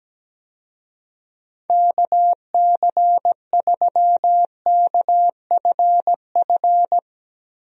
Morse code, heard as a keyed tone, about 17 wpm, 700 Hz. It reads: KC3KFF
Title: USA Call Signs - 6 Characters - Repeated 2x - 17wpm